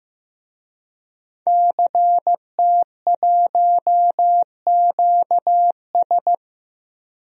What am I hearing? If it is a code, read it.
Morse code, heard as a keyed tone, about 15 wpm, 700 Hz. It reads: CT1QS
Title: DXCC-100 Call Signs - 5 Characters - 15wpm